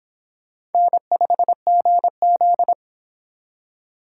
Morse code, heard as a keyed tone, about 29 wpm, 700 Hz. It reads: N5GZ